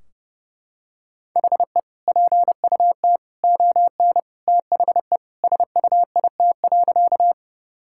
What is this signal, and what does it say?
Morse code, heard as a keyed tone, about 31 wpm, 700 Hz. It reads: HE PUT ON THE SUIT.
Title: Sentences from Top 1600 Words 30wpm